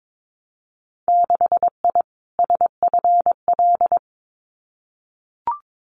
Morse code, heard as a keyed tone, about 22 wpm, 700 Hz. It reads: 6I SFL
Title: Winter Field Day Exchanges 22wpm